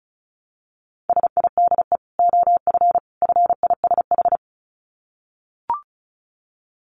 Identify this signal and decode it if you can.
Morse code, heard as a keyed tone, about 34 wpm, 700 Hz. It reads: SIDE OF FISH
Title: Dit Challenge 35wpm